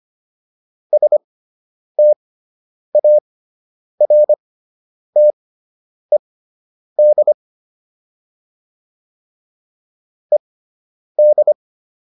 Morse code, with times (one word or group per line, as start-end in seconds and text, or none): 0.9-7.3 STARTED
10.3-11.5 ED